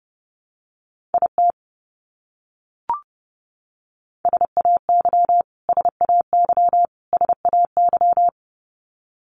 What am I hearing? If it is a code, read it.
Morse code, heard as a keyed tone, about 28 wpm, 700 Hz. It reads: IT SAY SAY SAY
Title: Top 300 Words Repeated 30wpm